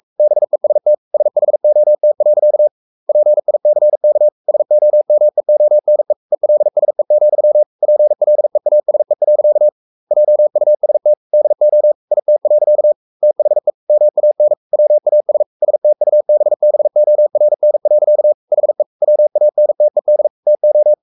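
Morse code, heard as a keyed tone, about 44 wpm, 600 Hz. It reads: BEST SHOT. PICK SOMEONE ELSE, PLEASE. JUST DO IT. THE MAN WAS STUBBORN. HE WANTED TO
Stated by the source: Morse code at 43 wpm sending Ken Follett's novel Never in English, from CW med Gnister.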